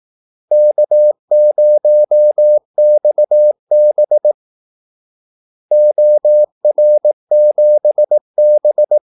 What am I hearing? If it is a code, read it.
Morse code, heard as a keyed tone, about 18 wpm, 600 Hz. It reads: K0XB OR7B
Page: Kaldesignaler 18 wpm